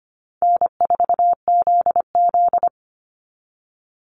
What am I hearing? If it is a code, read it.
Morse code, heard as a keyed tone, about 25 wpm, 700 Hz. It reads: N4ZZ
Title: Call Signs Repeated 3x - 25wpm